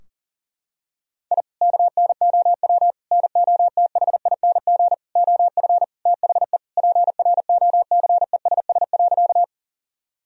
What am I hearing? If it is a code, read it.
Morse code, heard as a keyed tone, about 40 wpm, 700 Hz. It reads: I KNOW NOTHING OF THE PROCESS.
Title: Sentences from Top 1000 Words - Review Part 3 - 40wpm